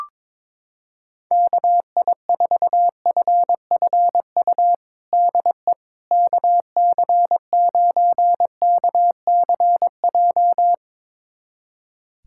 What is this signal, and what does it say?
Morse code, heard as a keyed tone, about 22 wpm, 700 Hz. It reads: KI4FFU DE KC9KCJ
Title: USA Callsign DE Callsign 22wpm